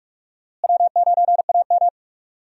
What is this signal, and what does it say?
Morse code, heard as a keyed tone, about 46 wpm, 700 Hz. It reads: W9AM
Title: USA Call Signs - 4 Characters - Repeated 3x - 45wpm